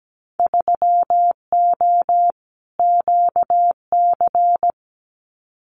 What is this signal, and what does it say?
Morse code, heard as a keyed tone, about 17 wpm, 700 Hz. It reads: 3O QC
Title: Winter Field Day Exchanges Encore 17wpm